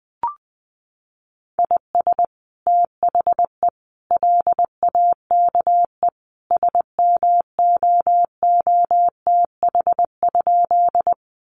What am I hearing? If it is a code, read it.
Morse code, heard as a keyed tone, about 20 wpm, 700 Hz. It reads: IS THE LAKE SMOOTH?
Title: Sentences from Top 1400 Words 20wpm